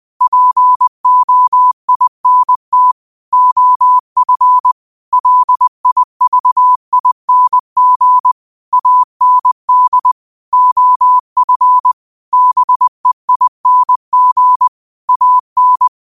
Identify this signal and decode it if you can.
Morse code, heard as a keyed tone, about 20 wpm, 1000 Hz. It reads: POINT OF LIVING AND OF BEING AN
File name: qotdfiles / 20wpm